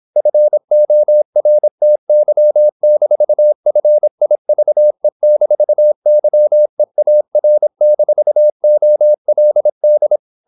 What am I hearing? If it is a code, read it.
Morse code, heard as a keyed tone, about 27 wpm, 600 Hz. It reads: FORTY-FIVE-YEAR-OLD
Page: Never 26wpm | CW med Gnister